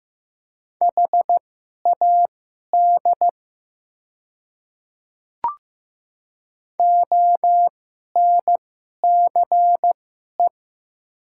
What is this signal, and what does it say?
Morse code, heard as a keyed tone, about 15 wpm, spaced as for 11 wpm, 700 Hz. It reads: HAD ONCE
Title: 2x Character Spacing - Top 300 Words - 15wpm